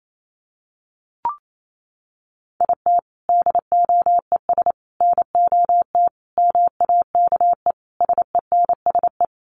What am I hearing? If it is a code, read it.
Morse code, heard as a keyed tone, about 28 wpm, 700 Hz. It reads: IT DOES NOT MAKE SENSE
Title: Sets of 5 Words Encore2 28wpm